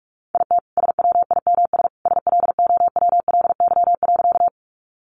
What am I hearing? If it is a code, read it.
Morse code, heard as a keyed tone, about 44 wpm, 700 Hz. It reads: IT SWIMS SLOWLY.
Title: Easy Phrases 45wpm